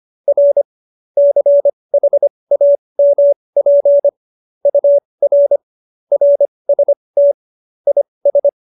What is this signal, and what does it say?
Morse code, heard as a keyed tone, about 25 wpm, spaced as for 20 wpm, 570 Hz. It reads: R CHAMP UR RST IS
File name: qso208b 20